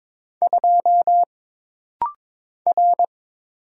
Morse code, heard as a keyed tone, about 22 wpm, 700 Hz. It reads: 2 R
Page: This rapid-fire timing is geared to push your ICR (Instant Character Recognition) skills.